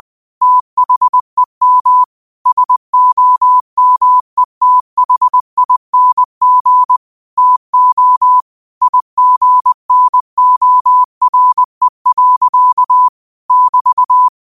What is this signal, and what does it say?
Morse code, heard as a keyed tone, about 20 wpm, 1000 Hz. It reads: THEM SOMETHING TO IGNORE. <BT>
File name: qotdfiles / 20wpm